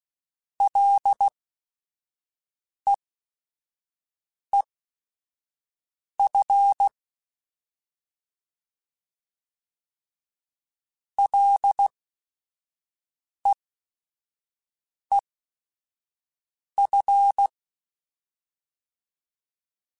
Morse code, heard as a keyed tone, about 16 wpm, spaced as for 5 wpm, 800 Hz. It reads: LEEF LEEF
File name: Short Runs / Short Runs-mp3s / SinglePrefixes @5